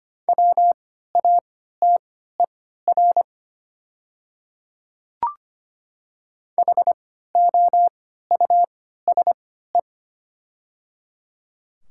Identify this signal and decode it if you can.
Morse code, heard as a keyed tone, about 25 wpm, spaced as for 14 wpm, 690 Hz. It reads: WATER HOUSE
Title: 3x Character Spacing - Top 300 Words - 25wpm